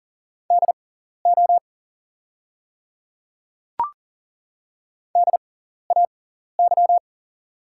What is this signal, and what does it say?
Morse code, heard as a keyed tone, about 40 wpm, spaced as for 14 wpm, 700 Hz. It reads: DO DAY
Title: 6x Character Spacing - Top 100 Words - 40wpm